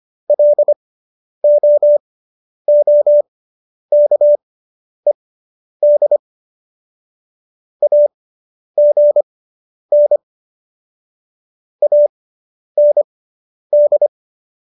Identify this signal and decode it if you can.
Morse code, heard as a keyed tone, about 25 wpm, spaced as for 10 wpm, 590 Hz. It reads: LOOKED AGN AND